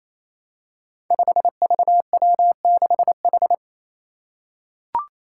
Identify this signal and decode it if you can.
Morse code, heard as a keyed tone, about 27 wpm, 700 Hz. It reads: 5VW6H
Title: Five Challenging Letter-Number Combinations 28wpm